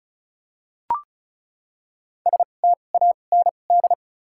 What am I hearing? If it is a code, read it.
Morse code, heard as a keyed tone, about 36 wpm, spaced as for 25 wpm, 700 Hz. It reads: STAND
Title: 2x Character Spacing - Top 300 Words - 35wpm